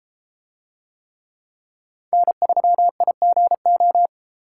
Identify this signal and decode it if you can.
Morse code, heard as a keyed tone, about 30 wpm, 700 Hz. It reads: N3IGO